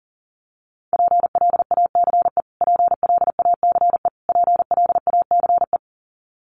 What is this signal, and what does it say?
Morse code, heard as a keyed tone, about 40 wpm, 700 Hz. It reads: PLACE PLACE PLACE
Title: Top 300 Words Repeated 40wpm